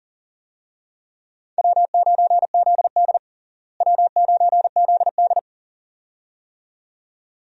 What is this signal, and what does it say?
Morse code, heard as a keyed tone, about 37 wpm, 700 Hz. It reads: W9ZD W9ZD
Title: USA Call Signs - 4 Characters - Repeated 2x - 40wpm